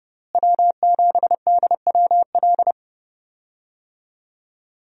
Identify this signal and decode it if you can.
Morse code, heard as a keyed tone, about 30 wpm, 700 Hz. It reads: W7DWL
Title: USA Call Signs - 5 Characters - Repeated 2x - 30wpm